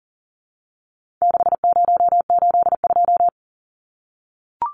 Morse code, heard as a keyed tone, about 40 wpm, 700 Hz. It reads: 6082
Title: Numbers - 4-Digit 40wpm